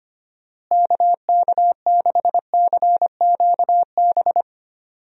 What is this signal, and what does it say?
Morse code, heard as a keyed tone, about 25 wpm, 700 Hz. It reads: KK6CQB
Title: USA Call Signs - 6 Characters - Repeated 2x - 25wpm